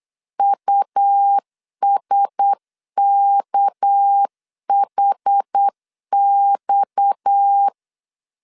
Morse code, heard as a keyed tone, about 8 wpm, 800 Hz. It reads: USKHX
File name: 8wpm